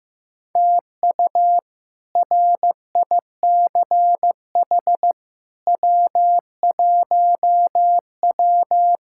Morse code, heard as a keyed tone, about 15 wpm, 700 Hz. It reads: TU RICH W1W
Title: K1USN SST QSO Practice - 15wpm